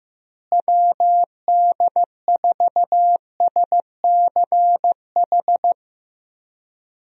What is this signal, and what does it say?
Morse code, heard as a keyed tone, about 15 wpm, 700 Hz. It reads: WD4SCH
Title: USA Call Signs - 6 Characters - Repeated 2x - 15wpm